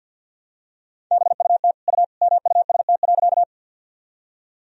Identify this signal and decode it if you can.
Morse code, heard as a keyed tone, about 50 wpm, 700 Hz. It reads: BUT U MUST.
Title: Sentences from Top 200 Words and Abbreviations 50wpm